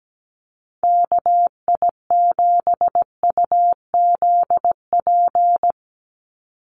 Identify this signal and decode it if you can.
Morse code, heard as a keyed tone, about 17 wpm, 700 Hz. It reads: KI7UZP
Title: USA Call Signs 17wpm